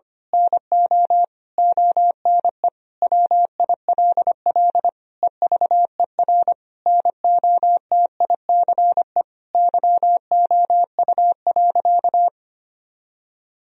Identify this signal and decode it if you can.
Morse code, heard as a keyed tone, about 25 wpm, 700 Hz. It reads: NO ONE WILL EVER NOTICE YOU.
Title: Sentences from Top 1000 Words - Review Part 4 - 25wpm